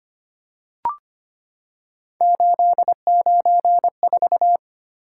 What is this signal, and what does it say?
Morse code, heard as a keyed tone, about 25 wpm, 700 Hz. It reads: 894
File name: Three Letter-Number Combinations - 25wpm